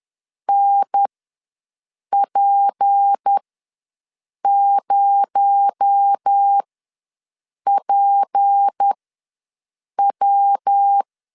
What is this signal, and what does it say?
Morse code, heard as a keyed tone, about 10 wpm, spaced as for 6 wpm, 800 Hz. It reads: NP0PW
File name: LEZIONI / 6 PKX0 / 6 PKX0 ws 10wpm 4220